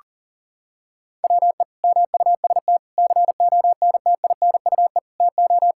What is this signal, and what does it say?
Morse code, heard as a keyed tone, about 37 wpm, 700 Hz. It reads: WE MUST CONTINUE TO
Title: Sentences from Top 1000 Words - Review Part 6 - 40wpm